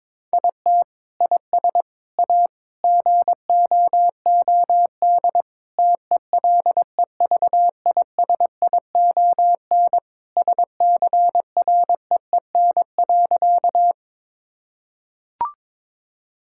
Morse code, heard as a keyed tone, about 21 wpm, 700 Hz. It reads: IT IS A GOOD TELEVISION SCREEN.
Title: Sentences from Top 1000 Words - Review Part 6 - 22wpm